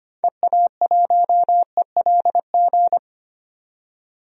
Morse code, heard as a keyed tone, about 25 wpm, 700 Hz. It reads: EA1ELG
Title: DXCC-100 Call Signs - 6 Characters - 25wpm